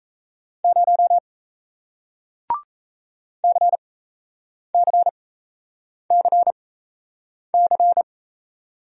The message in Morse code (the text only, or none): 0 CCCC